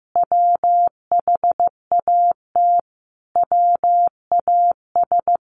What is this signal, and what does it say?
Morse code, heard as a keyed tone, about 15 wpm, 700 Hz. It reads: WHAT WAS
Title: Sentences from Top 1200 Words 15wpm